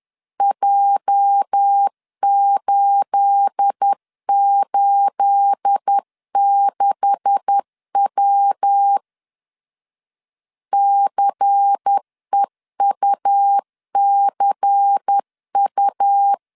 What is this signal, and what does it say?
Morse code, heard as a keyed tone, about 10 wpm, 800 Hz. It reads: J886W CEUCU
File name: CW / DET / 1 9 1500C S A 10wpm 4511